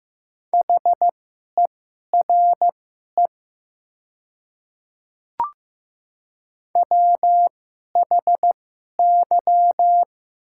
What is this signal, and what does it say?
Morse code, heard as a keyed tone, about 15 wpm, spaced as for 11 wpm, 690 Hz. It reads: HERE WHY
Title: 2x Character Spacing - Top 300 Words - 15wpm